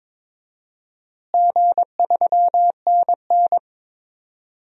Morse code, heard as a keyed tone, about 22 wpm, 700 Hz. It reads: G3NN
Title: DXCC-100 Call Signs - 4 Characters - Repeated 2x - 22wpm